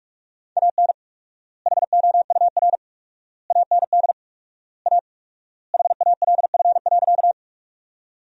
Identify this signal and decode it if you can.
Morse code, heard as a keyed tone, about 46 wpm, 700 Hz. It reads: AN HOUR AND A HALF.
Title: 4x Word-Spacing - Sentences from Top 400 Words - 45wpm